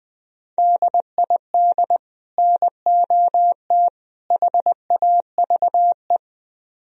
Morse code, heard as a keyed tone, about 22 wpm, 700 Hz. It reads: DID NOT HAVE
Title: Speed-Racing - Sets of 3 Words 20wpm